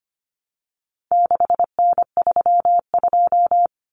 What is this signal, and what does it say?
Morse code, heard as a keyed tone, about 25 wpm, 700 Hz. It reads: 6N32
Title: Common Cut Numbers - 4-Digit 25wpm